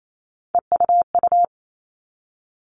Morse code, heard as a keyed tone, about 28 wpm, 700 Hz. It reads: EUU
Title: All Cut Numbers - 3-Digit 28wpm